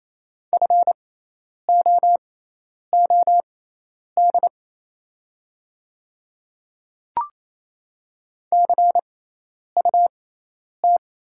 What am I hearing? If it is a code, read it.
Morse code, heard as a keyed tone, about 28 wpm, spaced as for 10 wpm, 700 Hz. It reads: FOOD CUT
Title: 6x Character Spacing - Top 500 Words - 28wpm